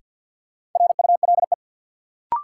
Morse code, heard as a keyed tone, about 53 wpm, 700 Hz. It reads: RULE
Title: Sentences from Top 400 Words - New Word Drill - 50wpm